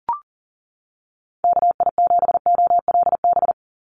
Morse code, heard as a keyed tone, about 40 wpm, 700 Hz. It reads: KI7OLB
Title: USA Call Signs 40wpm